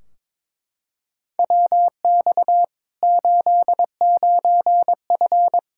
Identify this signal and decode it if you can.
Morse code, heard as a keyed tone, about 21 wpm, 700 Hz. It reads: WX 89F
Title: Practice QSO - Exchange 2 - Info - 22wpm